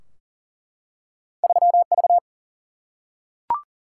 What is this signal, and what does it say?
Morse code, heard as a keyed tone, about 41 wpm, 700 Hz. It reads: 3 V
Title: Single-Letter Single-Number 40wpm